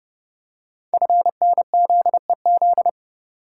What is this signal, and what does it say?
Morse code, heard as a keyed tone, about 30 wpm, 700 Hz. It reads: FNZEZ
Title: Code Groups - 5 Letters - 30wpm